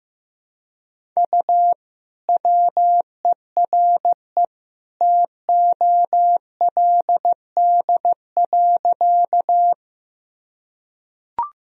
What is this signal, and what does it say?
Morse code, heard as a keyed tone, about 15 wpm, 700 Hz. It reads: U WERE TOLD.
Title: Sentences from Top 400 Words and Abbreviations 15wpm